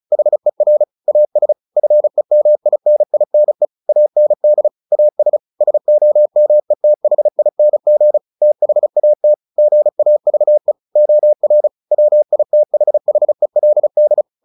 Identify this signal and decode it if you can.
Morse code, heard as a keyed tone, about 35 wpm, 600 Hz. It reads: HER AS FEMININE AND AS SOMETHING THAT GAVE OR WITHHELD
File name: the old man and the sea / the old man and the sea 1 35wpm